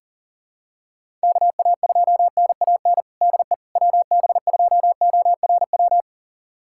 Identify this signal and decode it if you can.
Morse code, heard as a keyed tone, about 40 wpm, 700 Hz. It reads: KA2NAN DE WB2ORW
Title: USA Callsign DE Callsign 40wpm